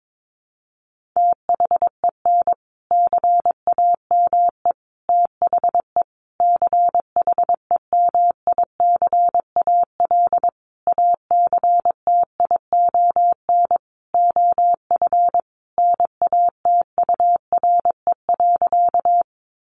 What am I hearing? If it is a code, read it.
Morse code, heard as a keyed tone, about 22 wpm, 700 Hz. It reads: THEN CAME THE CHEMICAL ACTION OF NATURE.
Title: Sentences from Top 1200 Words 22wpm